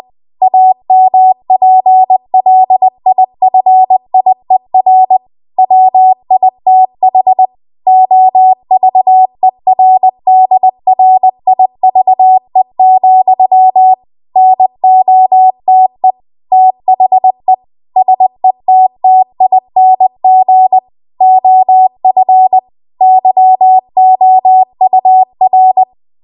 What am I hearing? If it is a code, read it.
Morse code, heard as a keyed tone, about 20 wpm, 750 Hz. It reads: AMPLIFIER WITH OVERDRIVE, NOTE THE SETTING OF YOUR
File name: Archive / 20 WPM / 20WPM